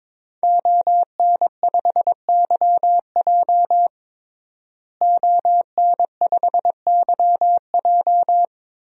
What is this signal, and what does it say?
Morse code, heard as a keyed tone, about 22 wpm, 700 Hz. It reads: ON5YJ ON5YJ
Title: DXCC-100 Call Signs - 5 Characters - Repeated 3x - 22wpm